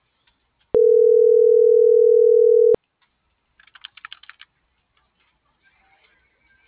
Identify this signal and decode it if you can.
Ringback tone (North America), 440 + 480 Hz.
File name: CENTRAL MAERSK / out isabel.bravo